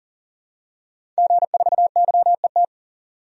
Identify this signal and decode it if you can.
Morse code, heard as a keyed tone, about 40 wpm, 700 Hz. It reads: G4YET
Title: DXCC-100 Call Signs - 5 Characters - 40wpm